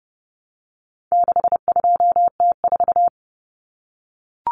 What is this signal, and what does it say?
Morse code, heard as a keyed tone, about 30 wpm, 700 Hz. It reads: 62T4
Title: Common Cut Numbers - 4-Digit 30wpm